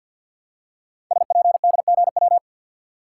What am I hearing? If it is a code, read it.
Morse code, heard as a keyed tone, about 46 wpm, 700 Hz. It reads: SPDGW
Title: Code Groups - 5 Letters - 50wpm